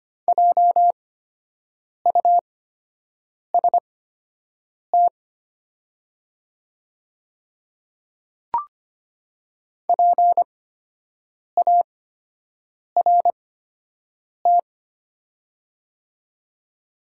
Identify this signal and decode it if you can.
Morse code, heard as a keyed tone, about 25 wpm, spaced as for 7 wpm, 700 Hz. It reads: JUST PART
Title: 8x Character Spacing - Top 300 Words - 25wpm